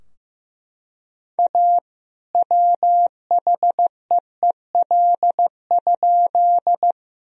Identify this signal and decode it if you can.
Morse code, heard as a keyed tone, about 16 wpm, 700 Hz. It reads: A WHEEL?
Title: Sentences from Top 1000 Words - Review Part 3 - 15wpm